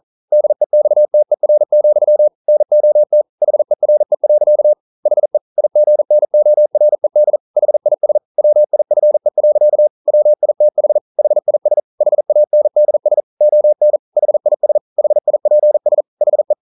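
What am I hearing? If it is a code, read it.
Morse code, heard as a keyed tone, about 42 wpm, 600 Hz. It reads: DEXTER, NOT HERE. HE IGNORED HIS WIFE. WITH HIS HANDS ON HIS HIPS HE
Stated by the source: Morse code practice at 41 wpm, from CW med Gnister.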